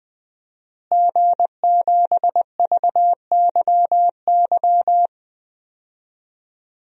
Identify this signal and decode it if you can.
Morse code, heard as a keyed tone, about 20 wpm, 700 Hz. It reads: G7VYY
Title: DXCC-100 Call Signs - 5 Characters - Repeated 2x - 20wpm